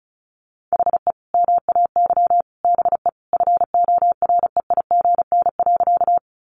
Morse code, heard as a keyed tone, about 34 wpm, 700 Hz. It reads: HE MAY BE FOREIGN.
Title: Sentences from Top 1400 Words 35wpm